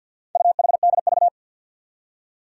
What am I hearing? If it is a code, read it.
Morse code, heard as a keyed tone, about 50 wpm, 700 Hz. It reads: UHDV
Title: Four Challenging Letter-Number Combinations 50wpm